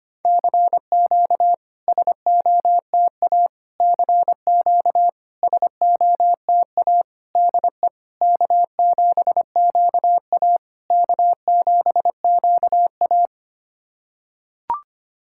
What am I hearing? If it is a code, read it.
Morse code, heard as a keyed tone, about 25 wpm, 700 Hz. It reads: CQ SOTA CQ SOTA DE K7QA K7QA
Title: SOTA QSO Practice 25wpm